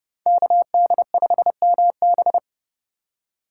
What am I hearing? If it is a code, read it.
Morse code, heard as a keyed tone, about 30 wpm, 700 Hz. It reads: KD5MB
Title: USA Call Signs - 5 Characters - 2x Character Spacing - 30wpm